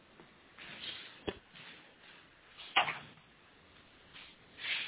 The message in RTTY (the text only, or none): none